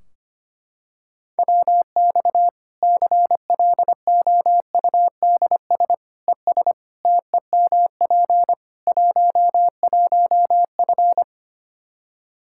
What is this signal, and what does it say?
Morse code, heard as a keyed tone, about 25 wpm, 700 Hz. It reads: WX CLOUDS ES TEMP 11F
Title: Practice QSO - Exchange 2 - Info - 25wpm